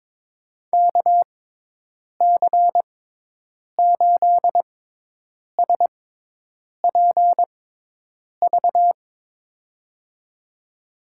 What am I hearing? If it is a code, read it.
Morse code, heard as a keyed tone, about 22 wpm, spaced as for 8 wpm, 700 Hz. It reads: KC8SPV
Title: USA Call Signs - 6x Character Spacing - 22wpm